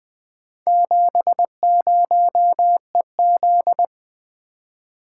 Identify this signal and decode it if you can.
Morse code, heard as a keyed tone, about 20 wpm, 700 Hz. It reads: 70EZ
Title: Four Letter-Number Combinations - 20wpm